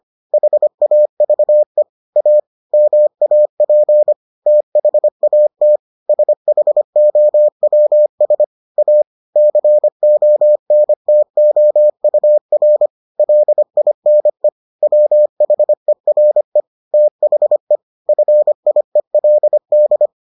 Morse code, heard as a keyed tone, about 25 wpm, 600 Hz. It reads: HAVE A MAP THAT SHOWS A CONTOUR LINE WHERE THE FIELD